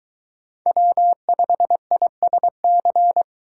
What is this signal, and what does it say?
Morse code, heard as a keyed tone, about 23 wpm, 700 Hz. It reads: W5ISC